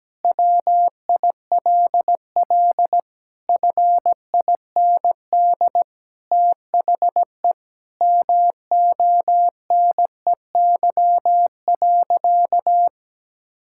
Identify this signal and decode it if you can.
Morse code, heard as a keyed tone, about 17 wpm, 700 Hz. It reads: WILL FIND THE MONEY.
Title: Sentences from Top 1600 Words 17wpm